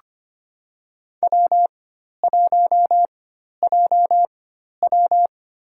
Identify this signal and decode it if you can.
Morse code, heard as a keyed tone, about 25 wpm, spaced as for 12 wpm, 700 Hz. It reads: W1JW
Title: USA Call Signs - 4 Characters - 4x Character Spacing - 25wpm